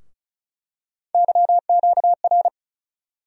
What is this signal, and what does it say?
Morse code, heard as a keyed tone, about 36 wpm, 700 Hz. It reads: YQR
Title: USA Call Sign Postfixes - 3 Letters 35wpm